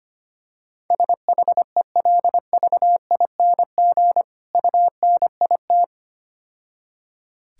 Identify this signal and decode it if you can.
Morse code, heard as a keyed tone, about 25 wpm, 700 Hz. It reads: SHELVING UNIT
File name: Easy Words - Speed-Racing with Hints - 25wpm